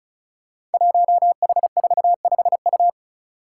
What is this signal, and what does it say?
Morse code, heard as a keyed tone, about 36 wpm, 700 Hz. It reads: 1H45U
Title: Five Challenging Letter-Number Combinations 35wpm